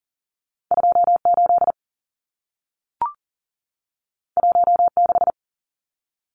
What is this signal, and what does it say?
Morse code, heard as a keyed tone, about 40 wpm, 700 Hz. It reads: 28 16
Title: Numbers - 2-Digit 40wpm